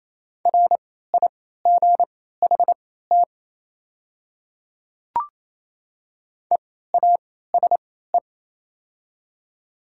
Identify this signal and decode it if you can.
Morse code, heard as a keyed tone, about 28 wpm, spaced as for 16 wpm, 700 Hz. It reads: RIGHT EASE
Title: 3x Character Spacing - Top 300 Words - 28wpm